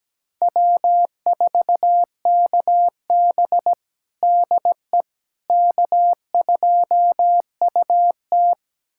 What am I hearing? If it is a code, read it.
Morse code, heard as a keyed tone, about 17 wpm, 700 Hz. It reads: W4KB DE K2UT
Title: USA Callsign DE Callsign - 4 Characters - 17wpm